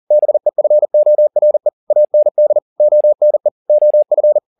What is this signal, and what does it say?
Morse code, heard as a keyed tone, about 40 wpm, 600 Hz. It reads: BEFORE AND ONE OF
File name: the old man and the sea 2 40wpm